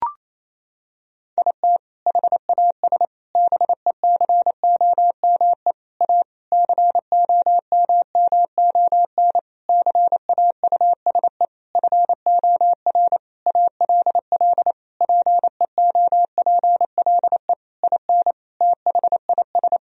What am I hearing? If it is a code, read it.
Morse code, heard as a keyed tone, about 28 wpm, 700 Hz. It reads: IT HAS BECOME A COMMON CAUSE FOR ALL PEOPLE IN THIS
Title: Sentences from Top 1000 Words - Review Part 3 - 28wpm